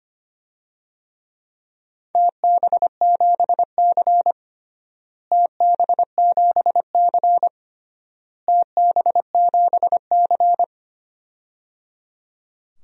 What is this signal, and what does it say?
Morse code, heard as a keyed tone, about 25 wpm, 700 Hz. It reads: TB7C TB7C TB7C
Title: DXCC-100 Call Signs - 4 Characters - Repeated 3x - 25wpm